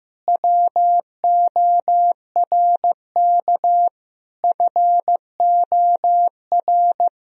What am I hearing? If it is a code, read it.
Morse code, heard as a keyed tone, about 15 wpm, 700 Hz. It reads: WORK FOR
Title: Sentences from Top 1000 Words - Review Part 2 - 15wpm